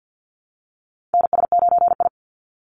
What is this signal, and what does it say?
Morse code, heard as a keyed tone, about 50 wpm, 700 Hz. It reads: NS9I
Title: Call Signs Repeated 3x - 50wpm